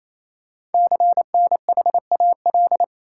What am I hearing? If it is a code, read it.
Morse code, heard as a keyed tone, about 27 wpm, 700 Hz. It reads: CNHAL
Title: Code Groups - 5 Letters - 28wpm